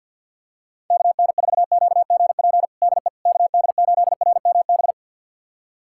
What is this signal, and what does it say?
Morse code, heard as a keyed tone, about 46 wpm, 700 Hz. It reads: KN4QGP DE KD8RMB
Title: USA Callsign DE Callsign 50wpm